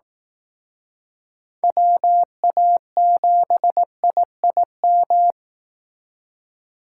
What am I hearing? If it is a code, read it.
Morse code, heard as a keyed tone, about 18 wpm, 700 Hz. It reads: WA7IIM
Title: Speed Racing - USA 2x3 Call Signs 15wpm